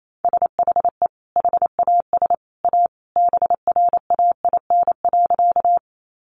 Morse code, heard as a keyed tone, about 28 wpm, 700 Hz. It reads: SHE HAS A BRAIN.
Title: Sentences from Top 1100 Words 28wpm